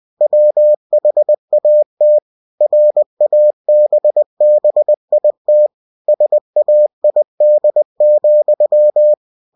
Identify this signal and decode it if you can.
Morse code, heard as a keyed tone, about 20 wpm, 600 Hz. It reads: WHAT RABBIT SAID,
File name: cw-books / winnie-the-pooh / winnie-20wpm-page-0062